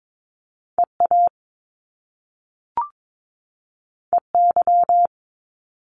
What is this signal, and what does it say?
Morse code, heard as a keyed tone, about 22 wpm, 700 Hz. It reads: EA EY
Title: Top Two Letter Combinations in Words 22wpm